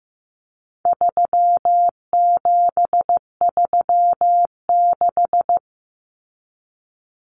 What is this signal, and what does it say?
Morse code, heard as a keyed tone, about 15 wpm, 700 Hz. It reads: 3736
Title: Numbers - 4-Digit 15wpm